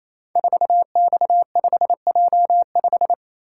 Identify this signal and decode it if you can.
Morse code, heard as a keyed tone, about 28 wpm, 690 Hz. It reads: 4X5J5
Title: Code Groups - 5 Letter:Numbers - 28wpm